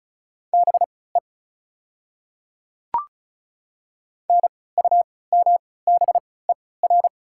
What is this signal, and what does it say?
Morse code, heard as a keyed tone, about 34 wpm, spaced as for 20 wpm, 700 Hz. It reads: BE NUMBER
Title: 3x Character Spacing - Top 100 Words - 35wpm